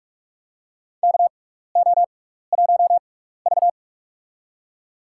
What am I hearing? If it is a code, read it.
Morse code, heard as a keyed tone, about 46 wpm, spaced as for 16 wpm, 700 Hz. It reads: KO1V
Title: USA Call Signs - 4 Characters - 6x Character Spacing - 45wpm